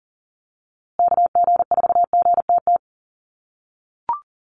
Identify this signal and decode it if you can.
Morse code, heard as a keyed tone, about 40 wpm, 700 Hz. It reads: KG4GTT
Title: USA Call Signs Repeated 2x 40wpm